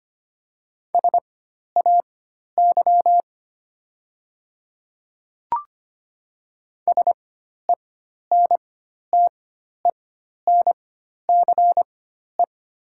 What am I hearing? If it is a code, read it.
Morse code, heard as a keyed tone, about 25 wpm, spaced as for 12 wpm, 700 Hz. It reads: SAY SENTENCE
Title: 4x Character Spacing - Top 200 Words - 25wpm